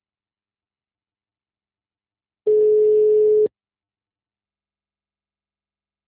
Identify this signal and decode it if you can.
Ringback tone (Europe, 425 Hz).
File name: out- -2012- - - .38874